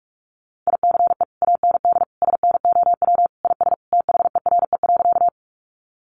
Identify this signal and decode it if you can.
Morse code, heard as a keyed tone, about 46 wpm, 700 Hz. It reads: ICE AND SNOW IS THERE.
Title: Sentences from Top 1000 Words 45wpm